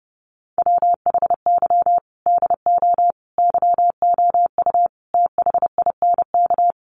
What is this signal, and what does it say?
Morse code, heard as a keyed tone, about 28 wpm, 700 Hz. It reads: WHY DO YOU THINK
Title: Sentences from Top 1400 Words 30wpm